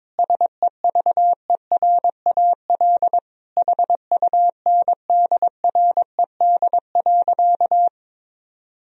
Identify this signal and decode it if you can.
Morse code, heard as a keyed tone, about 21 wpm, 700 Hz. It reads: SEVERAL HUNDRED.
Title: Sentences from Top 1000 Words - Review Part 2 - 22wpm